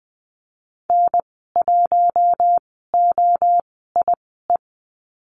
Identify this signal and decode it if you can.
Morse code, heard as a keyed tone, about 20 wpm, spaced as for 15 wpm, 700 Hz. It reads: N1OIE
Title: 088 Lesson 15g - Number 5 - Call Signs 20wpm